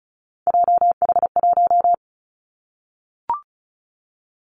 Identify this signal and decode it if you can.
Morse code, heard as a keyed tone, about 34 wpm, 700 Hz. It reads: JH1
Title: Callsign Prefixes for Top 25 DXCC - Repeated 3x - 35wpm